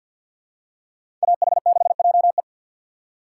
Morse code, heard as a keyed tone, about 53 wpm, 700 Hz. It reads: AH6JE
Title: USA Call Signs - 5 Characters - 2x Character Spacing - 50wpm